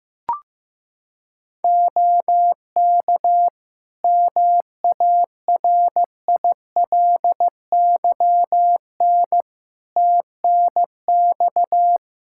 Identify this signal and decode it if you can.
Morse code, heard as a keyed tone, about 14 wpm, 700 Hz. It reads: OK MARILYN TNX
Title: Practice QSO - Exchange 4 - Ending - 15wpm